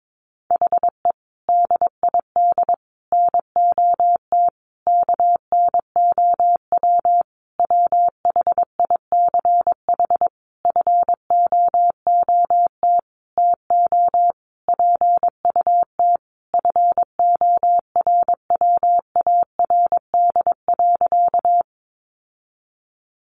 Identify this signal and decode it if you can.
Morse code, heard as a keyed tone, about 22 wpm, 700 Hz. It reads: HE DID NOT KNOW WHICH FOOT TO PUT FORWARD.
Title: Sentences from Top 1300 Words 22wpm